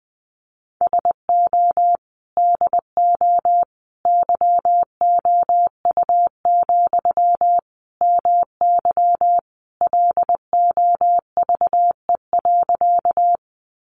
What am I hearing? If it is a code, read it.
Morse code, heard as a keyed tone, about 20 wpm, 700 Hz. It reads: SO DO YOU, MY LOVE.
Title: Sentences from Top 400 Words 20wpm